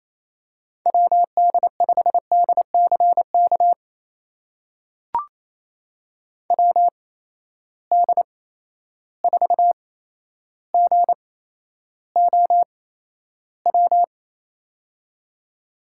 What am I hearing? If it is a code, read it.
Morse code, heard as a keyed tone, about 28 wpm, spaced as for 12 wpm, 700 Hz. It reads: WD5DCK WD4GOW
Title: USA Call Signs - 6 Characters - 8x Character Spacing - 28wpm